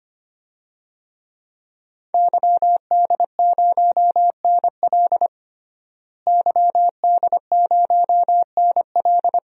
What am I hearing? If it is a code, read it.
Morse code, heard as a keyed tone, about 25 wpm, 700 Hz. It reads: YD0NL YD0NL
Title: DXCC-100 Call Signs - 5 Characters - Repeated 2x - 25wpm